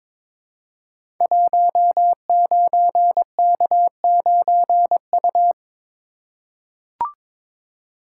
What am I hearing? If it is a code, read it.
Morse code, heard as a keyed tone, about 22 wpm, 700 Hz. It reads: 19K9U
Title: Code Groups - 5 Letter:Numbers - 22wpm